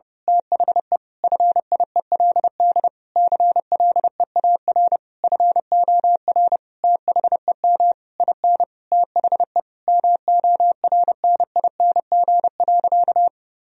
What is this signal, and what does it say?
Morse code, heard as a keyed tone, about 30 wpm, 700 Hz. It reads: THE FIELD CLEAR FOR THEM IN THE MORNING.
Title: Sentences from Top 1000 Words - Review Part 2 - 30wpm